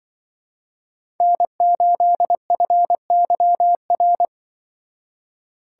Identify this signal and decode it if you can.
Morse code, heard as a keyed tone, about 25 wpm, 700 Hz. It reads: N8FYR